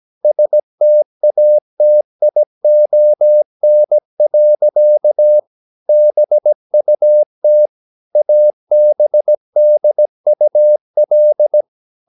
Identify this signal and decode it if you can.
Morse code, heard as a keyed tone, about 17 wpm, 600 Hz. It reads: STATION. BUT ABDUL
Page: Romanen Never af Ken Follet på engelsk.